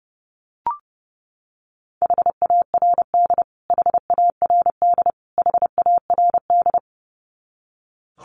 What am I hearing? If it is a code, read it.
Morse code, heard as a keyed tone, about 30 wpm, 700 Hz. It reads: HARD HARD HARD
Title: Top 300 Words Repeated 30wpm